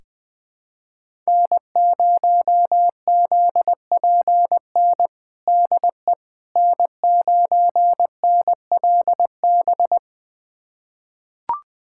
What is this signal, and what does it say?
Morse code, heard as a keyed tone, about 20 wpm, 700 Hz. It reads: N0ZPN DE N9NLB
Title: USA Callsign DE Callsign 20wpm